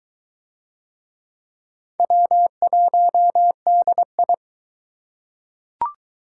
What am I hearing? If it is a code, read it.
Morse code, heard as a keyed tone, about 19 wpm, 700 Hz. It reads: W1DI